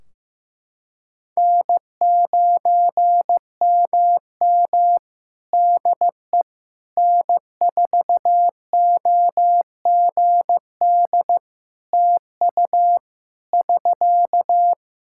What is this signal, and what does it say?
Morse code, heard as a keyed tone, about 16 wpm, 700 Hz. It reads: N9MM DE N4OGD TU <SK>
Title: K1USN SST QSO Practice 15wpm